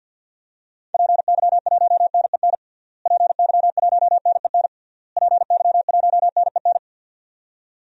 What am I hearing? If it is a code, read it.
Morse code, heard as a keyed tone, about 53 wpm, 700 Hz. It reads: PY1NEN PY1NEN PY1NEN
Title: DXCC-100 Call Signs - 6 Characters - Repeated 3x - 50wpm